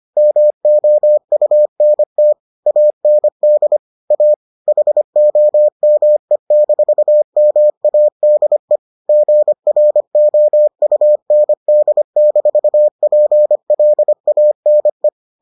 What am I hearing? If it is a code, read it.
Morse code, heard as a keyed tone, about 25 wpm, 600 Hz. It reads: MOUNT AND A HOME-MADE GROUND-PLANE